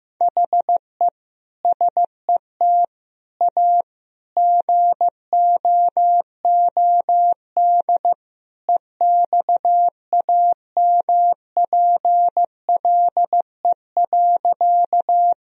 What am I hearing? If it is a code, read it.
Morse code, heard as a keyed tone, about 15 wpm, 700 Hz. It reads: HE SET A GOOD EXAMPLE.
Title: Sentences from Top 1000 Words - Review Part 1 - 15wpm